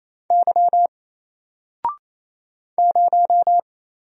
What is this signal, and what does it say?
Morse code, heard as a keyed tone, about 27 wpm, 700 Hz. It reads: Y 0
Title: Single Letter-Number Alt 28wpm